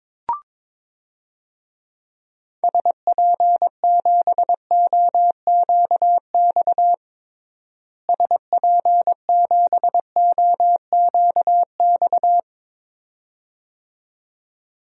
Morse code, heard as a keyed tone, about 21 wpm, 700 Hz. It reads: SP7OQX SP7OQX
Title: DXCC-100 Call Signs - 6 Characters - Repeated 2x - 22wpm